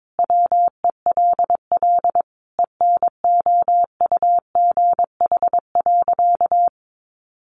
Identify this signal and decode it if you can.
Morse code, heard as a keyed tone, about 22 wpm, 700 Hz. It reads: WELL ENOUGH.
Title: Sentences from Top 400 Words 22wpm